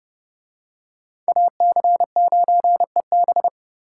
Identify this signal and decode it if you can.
Morse code, heard as a keyed tone, about 27 wpm, 700 Hz. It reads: AC9EB